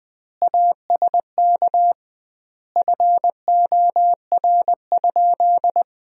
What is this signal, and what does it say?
Morse code, heard as a keyed tone, about 20 wpm, 700 Hz. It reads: ASK FOR?
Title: 2x Word-Spacing - Sentences from Top 200 Words - 20wpm